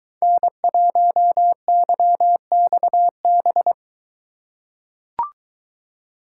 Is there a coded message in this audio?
Yes: Morse code from 0.2 to 3.7 s, heard as a keyed tone, about 23 wpm, 700 Hz. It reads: N1YXB